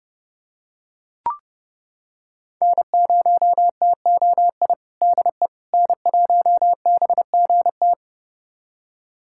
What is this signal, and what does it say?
Morse code, heard as a keyed tone, about 32 wpm, 700 Hz. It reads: N0TOI DE N1BGT
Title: USA Callsign DE Callsign - 5 Characters - 30wpm